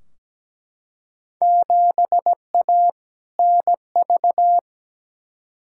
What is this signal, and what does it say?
Morse code, heard as a keyed tone, about 17 wpm, 700 Hz. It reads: 7A NV
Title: Field Day Exchanges Encore - 17wpm